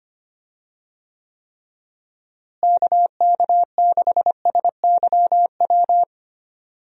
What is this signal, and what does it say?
Morse code, heard as a keyed tone, about 25 wpm, 700 Hz. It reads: KK6SYW
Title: USA Call Signs - 6 Characters - Repeated 3x - 25wpm